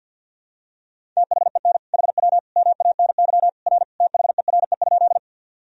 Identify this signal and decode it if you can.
Morse code, heard as a keyed tone, about 50 wpm, 700 Hz. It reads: THEN HW MANY R THERE?
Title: Sentences from Top 100 Words and Abbreviations 50wpm